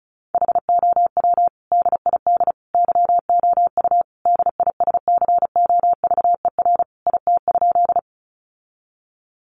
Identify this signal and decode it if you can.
Morse code, heard as a keyed tone, about 34 wpm, 700 Hz. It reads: HOW DID YOU DISCOVER IT?
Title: Sentences from Top 1100 Words 35wpm